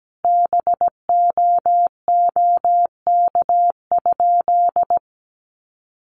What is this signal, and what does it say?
Morse code, heard as a keyed tone, about 17 wpm, 700 Hz. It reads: BOOK?
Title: Sentences from Top 1300 Words 17wpm